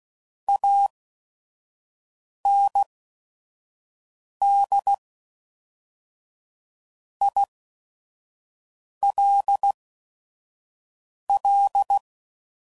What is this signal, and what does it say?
Morse code, heard as a keyed tone, about 16 wpm, spaced as for 5 wpm, 800 Hz. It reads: AND ILL
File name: CW Code Groups @5WPM / QSOs / kc7sac